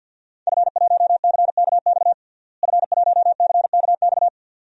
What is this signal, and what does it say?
Morse code, heard as a keyed tone, about 46 wpm, 700 Hz. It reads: F1CKX F1CKX
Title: DXCC-100 Call Signs - 5 Characters - Repeated 2x - 50wpm